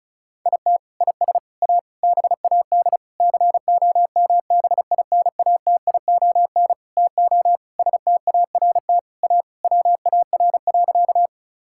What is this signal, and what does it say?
Morse code, heard as a keyed tone, about 36 wpm, 700 Hz. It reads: IT IS A BAD COMBINATION TO START A WAR.
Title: Sentences from Top 1000 Words - Review Part 6 - 35wpm